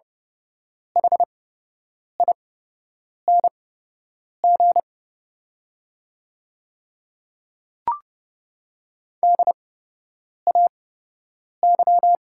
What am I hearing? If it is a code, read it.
Morse code, heard as a keyed tone, about 30 wpm, spaced as for 8 wpm, 700 Hz. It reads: HING DAY
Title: 8x Character Spacing - Top 100 Words - 30wpm